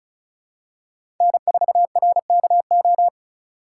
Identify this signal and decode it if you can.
Morse code, heard as a keyed tone, about 36 wpm, 700 Hz. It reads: N4RKO
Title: USA Call Signs - 5 Characters - 3x Character Spacing - 35wpm